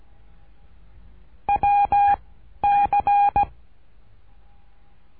Morse code, heard as a keyed tone, about 17 wpm, 810 Hz. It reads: W C
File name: cursos / CURSO DE TELEGRAFIA / Telegrafia - Trilha 13